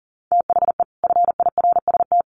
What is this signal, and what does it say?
Morse code, heard as a keyed tone, about 40 wpm, 700 Hz. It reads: THE FIRST
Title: Sets of 5 Words Encore 40wpm